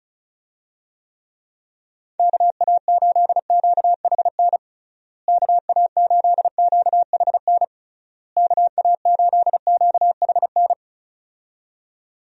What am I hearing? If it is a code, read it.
Morse code, heard as a keyed tone, about 34 wpm, 700 Hz. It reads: KA8QHN KA8QHN KA8QHN
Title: USA Call Signs - 6 Characters - Repeated 3x - 35wpm